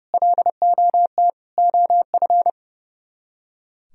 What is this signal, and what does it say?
Morse code, heard as a keyed tone, about 36 wpm, 700 Hz. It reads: LOT OF
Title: Speed-Racing - Sets of 5 Words 30wpm